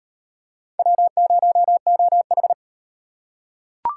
Morse code, heard as a keyed tone, about 34 wpm, 700 Hz. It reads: W0OH